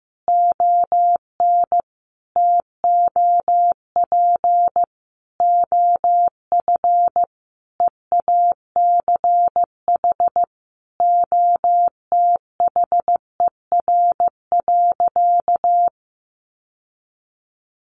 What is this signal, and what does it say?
Morse code, heard as a keyed tone, about 15 wpm, 700 Hz. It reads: ON TOP OF EACH OTHER.
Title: Sentences from Top 700 Words 15wpm